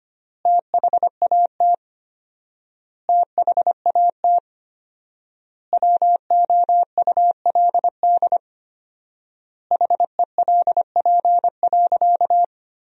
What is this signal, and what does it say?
Morse code, heard as a keyed tone, about 25 wpm, 700 Hz. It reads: THAT THAT WOULD HELP.
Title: 4x Word-Spacing - Sentences from Top 200 Words - 25wpm